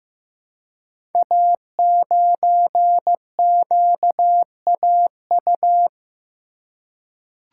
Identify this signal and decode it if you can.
Morse code, heard as a keyed tone, about 15 wpm, 700 Hz. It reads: A9QAU
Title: Code Groups - 5 Letter:Numbers - 15wpm